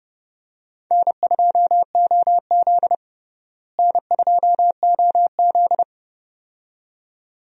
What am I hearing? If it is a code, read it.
Morse code, heard as a keyed tone, about 30 wpm, 700 Hz. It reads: N2OZ N2OZ
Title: USA Call Signs - 4 Characters - Repeated 2x - 30wpm